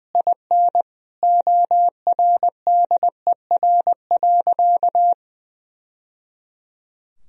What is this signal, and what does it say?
Morse code, heard as a keyed tone, about 21 wpm, 700 Hz. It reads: IN ORDER.
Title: Sentences from Top 1000 Words - Review Part 2 - 20wpm